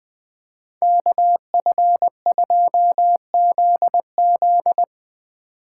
Morse code, heard as a keyed tone, about 24 wpm, 700 Hz. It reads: KF2ZZ
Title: Speed Racing - USA 2x2 Call Signs 20wpm